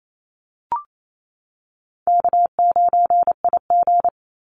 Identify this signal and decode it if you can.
Morse code, heard as a keyed tone, about 28 wpm, 700 Hz. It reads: K9IG
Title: Call Signs - 28wpm